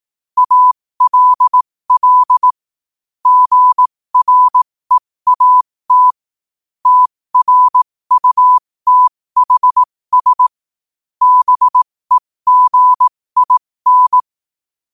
Morse code, heard as a keyed tone, about 18 wpm, spaced as for 15 wpm, 1000 Hz. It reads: ALL GREAT TRUTHS BEGIN
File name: qotdfiles / 15wpm